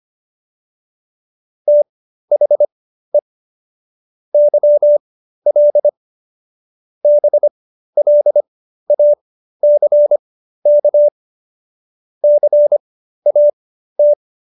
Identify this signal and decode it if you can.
Morse code, heard as a keyed tone, about 25 wpm, spaced as for 13 wpm, 590 Hz. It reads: THE YL BLACK CAT